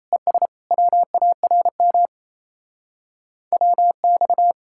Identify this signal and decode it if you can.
Morse code, heard as a keyed tone, about 34 wpm, 700 Hz. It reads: ES WARM WX
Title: Speed-Racing Common QSO Phrases - 28wpm